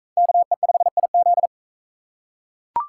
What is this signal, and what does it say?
Morse code, heard as a keyed tone, about 38 wpm, 700 Hz. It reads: KE5IZ